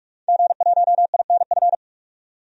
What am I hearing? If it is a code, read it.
Morse code, heard as a keyed tone, about 43 wpm, 700 Hz. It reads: G1INF
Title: DXCC-100 Call Signs - 5 Characters - 45wpm